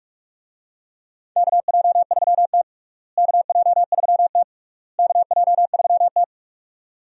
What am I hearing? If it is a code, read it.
Morse code, heard as a keyed tone, about 48 wpm, 700 Hz. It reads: KJ3T KJ3T KJ3T
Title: USA Call Signs - 4 Characters - Repeated 3x - 45wpm